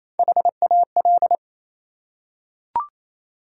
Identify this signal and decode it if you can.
Morse code, heard as a keyed tone, about 28 wpm, 700 Hz. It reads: HAL
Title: USA Common Names - Max 5 Letters - 28wpm